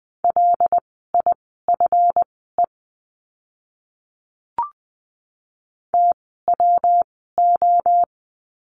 Morse code, heard as a keyed tone, about 20 wpm, spaced as for 14 wpm, 700 Hz. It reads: LIFE TWO
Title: 139 Lesson 23f - Letter f - Words 20wpm